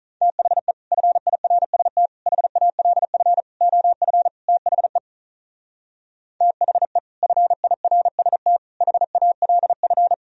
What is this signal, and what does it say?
Morse code, heard as a keyed tone, about 44 wpm, 700 Hz. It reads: THE FIRST HALF OF THE THE FIRST HALF